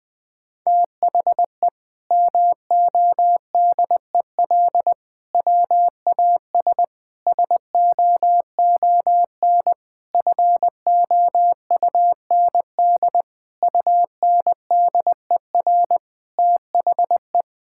Morse code, heard as a keyed tone, about 20 wpm, 700 Hz. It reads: THE MODEL WAS SOON FOUND UNDER THE
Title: Sentences from Top 1000 Words - Review Part 5 - 20wpm